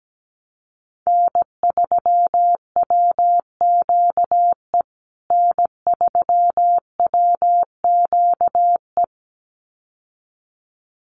Morse code, heard as a keyed tone, about 17 wpm, 700 Hz. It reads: N3WQE N3WQE
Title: USA Call Signs Repeated 2x 17wpm